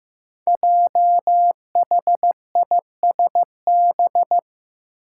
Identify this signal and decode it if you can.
Morse code, heard as a keyed tone, about 15 wpm, 700 Hz. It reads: JHISB
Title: Five Challenging Letter-Number Combinations 15wpm